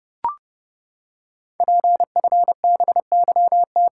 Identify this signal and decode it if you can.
Morse code, heard as a keyed tone, about 28 wpm, 700 Hz. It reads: PFBYT
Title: Code Groups - 5 Letters - 30wpm